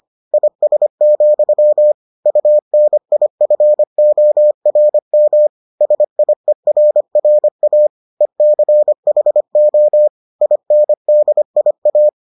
Morse code, heard as a keyed tone, about 25 wpm, 600 Hz. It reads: IS, UNIFORM SIERRA ECHO INDIA